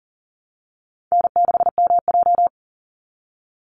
Morse code, heard as a keyed tone, about 38 wpm, 700 Hz. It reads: N6MJ
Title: Easy Call Signs - 40wpm